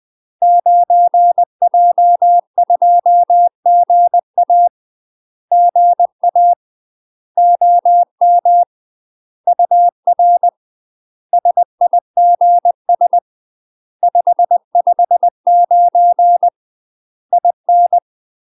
Morse code, heard as a keyed tone, about 20 wpm, 700 Hz. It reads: 9J2GA GA OM UR SIGS 559 IN
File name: UK MASTER QSOs TRAINING / 20WPM / UK MASTER QSOs 0035